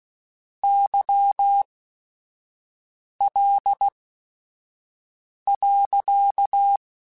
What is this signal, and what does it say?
Morse code, heard as a keyed tone, about 16 wpm, spaced as for 5 wpm, 800 Hz. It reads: YL.